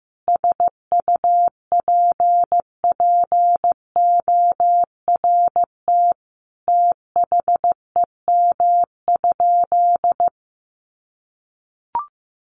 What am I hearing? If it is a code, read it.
Morse code, heard as a keyed tone, about 15 wpm, 700 Hz. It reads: SUPPORT THEM?
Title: Sentences from Top 800 Words 15wpm